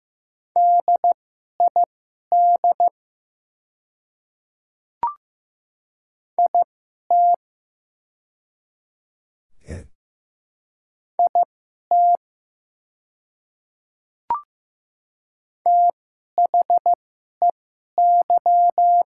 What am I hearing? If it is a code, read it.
Morse code, heard as a keyed tone, about 15 wpm, spaced as for 11 wpm, 690 Hz. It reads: DID IT IT THEY
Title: 2x Character Spacing - Top 100 Words - 15wpm